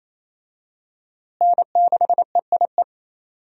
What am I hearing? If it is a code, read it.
Morse code, heard as a keyed tone, about 28 wpm, 700 Hz. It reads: N6EIE
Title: USA Call Signs - 5 Characters - Repeated 3x - 28wpm